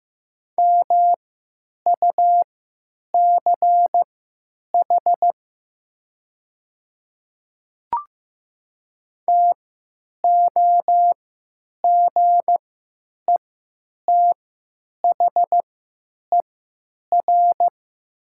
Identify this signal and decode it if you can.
Morse code, heard as a keyed tone, about 15 wpm, spaced as for 8 wpm, 700 Hz. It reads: MUCH TOGETHER
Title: 3x Character Spacing - Top 500 Words - 15wpm